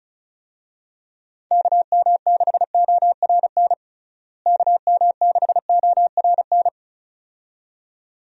Morse code, heard as a keyed tone, about 36 wpm, 700 Hz. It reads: KM6ORN KM6ORN
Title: USA Call Signs - 6 Characters - Repeated 2x - 35wpm